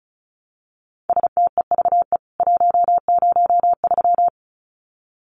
Morse code, heard as a keyed tone, about 34 wpm, 700 Hz. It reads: STEVE 103
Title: CWT Exchange 35wpm